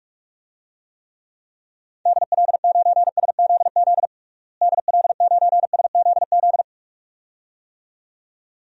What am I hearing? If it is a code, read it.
Morse code, heard as a keyed tone, about 47 wpm, 700 Hz. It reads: DL9SZZ DL9SZZ
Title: DXCC-100 Call Signs - 6 Characters - Repeated 2x - 45wpm